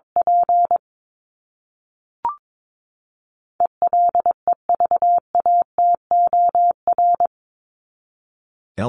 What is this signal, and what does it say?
Morse code, heard as a keyed tone, about 22 wpm, 700 Hz. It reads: P ELEVATOR
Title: Top 1501-1750 Words 22wpm